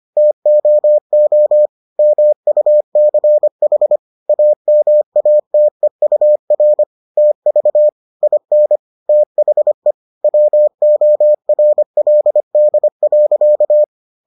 Morse code, heard as a keyed tone, about 25 wpm, 600 Hz. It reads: TOO MUCH AMATEUR TV IN THE WORLD.